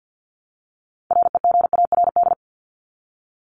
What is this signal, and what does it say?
Morse code, heard as a keyed tone, about 49 wpm, 700 Hz. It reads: REGARD
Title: ING Suffix - Top 100 Words with Base Words 50wpm